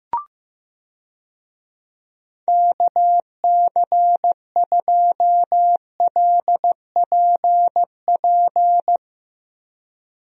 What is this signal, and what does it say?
Morse code, heard as a keyed tone, about 15 wpm, 700 Hz. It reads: KC2LPP
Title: USA Call Signs - 6 Characters - Repeated 3x - 15wpm